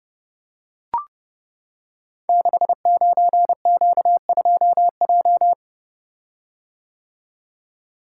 Morse code, heard as a keyed tone, about 28 wpm, 700 Hz. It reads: 69Q2J
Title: Code Groups - 5 Letter:Numbers - 30wpm